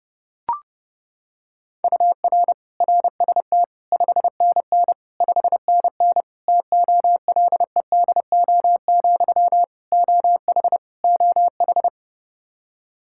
Morse code, heard as a keyed tone, about 32 wpm, 700 Hz. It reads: UR RST 5NN 5NN TOLEDO, OH OH
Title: Practice QSO - Exchange 1 - Beginning - 30wpm